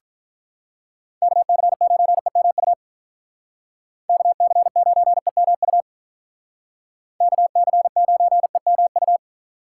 Morse code, heard as a keyed tone, about 45 wpm, 700 Hz. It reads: KC9EMU KC9EMU KC9EMU